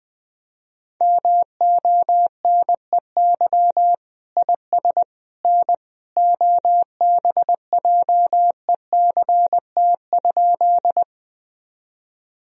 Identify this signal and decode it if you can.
Morse code, heard as a keyed tone, about 20 wpm, 700 Hz. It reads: MONEY IS N OBJECT?
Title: Sentences from Top 500 Words and Abbreviations 20wpm